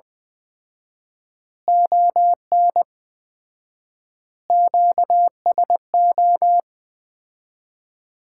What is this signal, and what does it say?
Morse code, heard as a keyed tone, about 20 wpm, 700 Hz. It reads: ON QSO
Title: Common QSO Phrases - 4x Word-Spacing - 20wpm